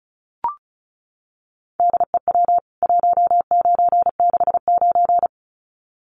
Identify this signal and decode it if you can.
Morse code, heard as a keyed tone, about 34 wpm, 700 Hz. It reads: DEW 1969
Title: CWT Exchange 35wpm